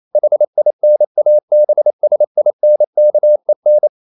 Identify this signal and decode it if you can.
Morse code, heard as a keyed tone, about 28 wpm, 600 Hz. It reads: HINABSINKEN
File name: marienkind9 28wpm